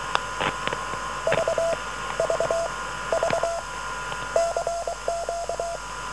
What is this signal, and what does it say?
Morse code, heard as a keyed tone, about 23 wpm, spaced as for 15 wpm, 670 Hz, with noise. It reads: VVV CQ